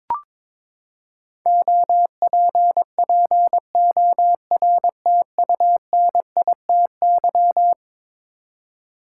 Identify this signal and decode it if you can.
Morse code, heard as a keyed tone, about 21 wpm, 700 Hz. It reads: OPPORTUNITY
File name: practice / 22wpm / Eleven Letter Words - 22wpm